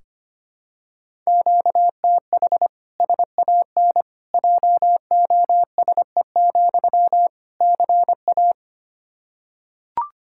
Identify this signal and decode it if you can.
Morse code, heard as a keyed tone, about 25 wpm, 700 Hz. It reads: QTH SAN JOSE, CA
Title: Practice QSO - Exchange 1 - Beginning - 25wpm